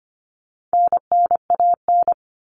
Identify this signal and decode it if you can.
Morse code, heard as a keyed tone, about 25 wpm, 700 Hz. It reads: NNAN
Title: All Cut Numbers - 4-Digit 25wpm